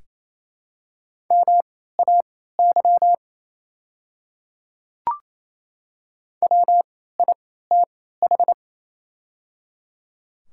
Morse code, heard as a keyed tone, about 28 wpm, spaced as for 16 wpm, 700 Hz. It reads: MAY WITH
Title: 3x Character Spacing - Top 500 Words - 28wpm